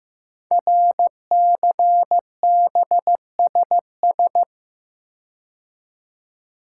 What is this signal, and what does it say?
Morse code, heard as a keyed tone, about 15 wpm, 700 Hz. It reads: RCBSS
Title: Code Groups - 5 Letters - 15wpm